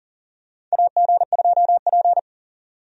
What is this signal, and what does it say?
Morse code, heard as a keyed tone, about 40 wpm, 700 Hz. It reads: AG2P
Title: USA Call Signs - 4 Characters - 40wpm